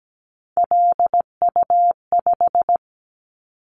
Morse code, heard as a keyed tone, about 17 wpm, 700 Hz. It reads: LU5
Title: Callsign Prefixes for Top 100 DXCC - 17wpm